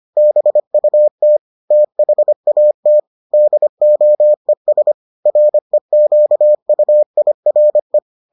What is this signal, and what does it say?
Morse code, heard as a keyed tone, about 25 wpm, 600 Hz. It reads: BUT THAT DOES REQUIRE